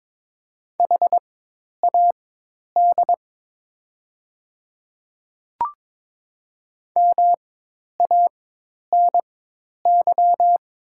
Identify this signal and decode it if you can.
Morse code, heard as a keyed tone, about 22 wpm, spaced as for 10 wpm, 700 Hz. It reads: HAD MANY
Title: 4x Character Spacing - Top 100 Words - 22wpm